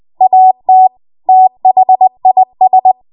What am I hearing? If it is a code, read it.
Morse code, heard as a keyed tone, about 20 wpm, 750 Hz. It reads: AT THIS
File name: Morse / 20WPM